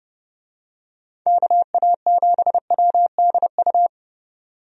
Morse code, heard as a keyed tone, about 30 wpm, 700 Hz. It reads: KA7WDU
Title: USA Call Signs - 6 Characters - 30wpm